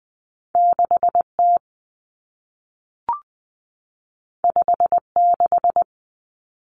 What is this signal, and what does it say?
Morse code, heard as a keyed tone, about 20 wpm, 700 Hz. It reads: 6T 56
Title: Common Cut Numbers - 2-Digit 20wpm